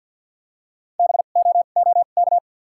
Morse code, heard as a keyed tone, about 52 wpm, spaced as for 36 wpm, 700 Hz. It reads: BOOK
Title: 2x Character Spacing - Top 500 Words - 50wpm